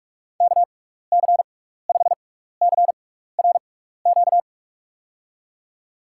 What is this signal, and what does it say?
Morse code, heard as a keyed tone, about 46 wpm, spaced as for 16 wpm, 700 Hz. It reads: KC5CRQ
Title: USA Call Signs - 6 Characters - 6x Character Spacing - 45wpm